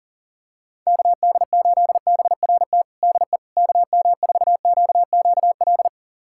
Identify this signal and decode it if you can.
Morse code, heard as a keyed tone, about 40 wpm, 700 Hz. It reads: KD8BRT DE KM4QQL
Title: USA Callsign DE Callsign 40wpm